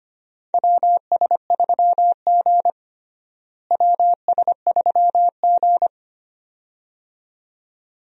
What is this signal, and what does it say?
Morse code, heard as a keyed tone, about 25 wpm, 700 Hz. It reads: WS3G WS3G
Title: USA Call Signs - 4 Characters - Repeated 2x - 25wpm